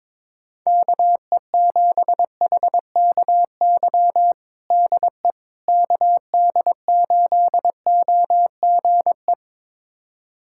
Morse code, heard as a keyed tone, about 21 wpm, 700 Hz. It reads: KE7HKY DE KD8OGE
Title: USA Callsign DE Callsign - 6 Characters - 22wpm